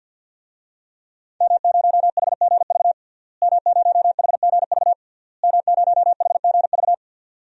Morse code, heard as a keyed tone, about 52 wpm, 700 Hz. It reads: M0HGV M0HGV M0HGV
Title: DXCC-100 Call Signs - 5 Characters - Repeated 3x - 50wpm